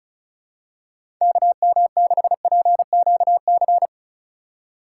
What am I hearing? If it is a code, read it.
Morse code, heard as a keyed tone, about 36 wpm, 700 Hz. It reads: KM6PQC
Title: USA Call Signs - 6 Characters - 3x Character Spacing - 35wpm